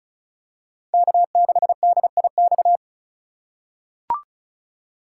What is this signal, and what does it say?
Morse code, heard as a keyed tone, about 34 wpm, 700 Hz. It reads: K6DIX
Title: USA Call Signs - 5 Characters - 6x Character Spacing - 35wpm